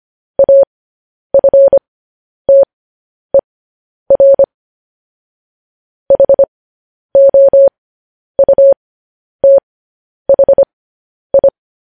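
Morse code, heard as a keyed tone, about 25 wpm, spaced as for 10 wpm, 550 Hz. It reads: AFTER HOUTHI
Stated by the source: Morse code news headlines sent at 10 wpm.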